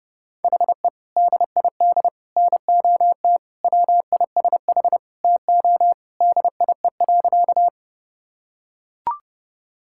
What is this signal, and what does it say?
Morse code, heard as a keyed tone, about 30 wpm, 700 Hz. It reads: HE DID NOT WISH TO DIE.
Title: Sentences from Top 1600 Words 30wpm